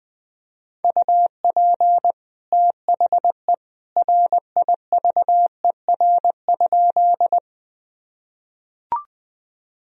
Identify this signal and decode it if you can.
Morse code, heard as a keyed tone, about 21 wpm, 700 Hz. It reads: UP THE RIVER?
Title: Sentences from Top 1000 Words - Review Part 1 - 20wpm